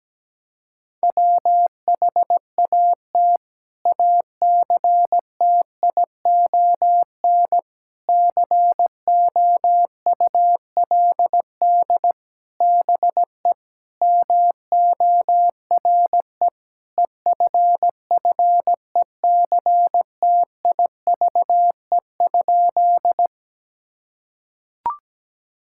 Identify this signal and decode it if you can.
Morse code, heard as a keyed tone, about 17 wpm, 700 Hz. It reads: WHAT ACTION COULD BE MORE EFFECTIVE?
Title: Sentences from Top 1000 Words - Review Part 6 - 17wpm